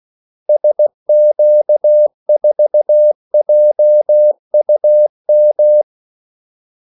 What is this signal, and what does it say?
Morse code, heard as a keyed tone, about 16 wpm, 600 Hz. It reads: SQ4JUM
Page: Kaldesignaler 16 wpm